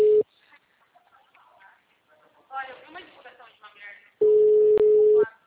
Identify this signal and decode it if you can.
Ringback tone (Europe, 425 Hz), with speech.